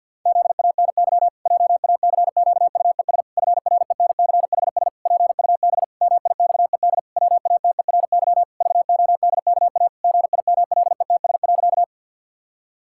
Morse code, heard as a keyed tone, about 50 wpm, 700 Hz. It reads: ZANY JACQUES FRENCHS PUB MIXED WATERY VODKA GIMLETS.
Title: Pangram Challenge v2 - 50wpm